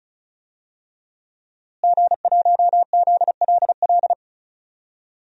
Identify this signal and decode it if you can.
Morse code, heard as a keyed tone, about 36 wpm, 700 Hz. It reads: G1ZLL
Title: DXCC-100 Call Signs - 5 Characters - Repeated 3x - 35wpm